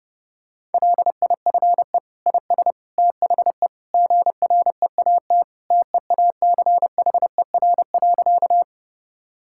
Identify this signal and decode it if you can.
Morse code, heard as a keyed tone, about 30 wpm, 700 Hz. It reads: LIFE IS THE GREAT TEACHER.
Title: Sentences from Top 1000 Words - Review Part 6 - 30wpm